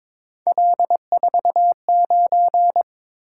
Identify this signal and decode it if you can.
Morse code, heard as a keyed tone, about 21 wpm, 700 Hz. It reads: L49
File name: practice / 22wpm / Three Letter-Number Combinations - 22wpm